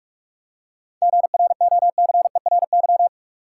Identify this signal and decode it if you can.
Morse code, heard as a keyed tone, about 46 wpm, 700 Hz. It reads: GROCERY
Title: Seven Letter Words - 45wpm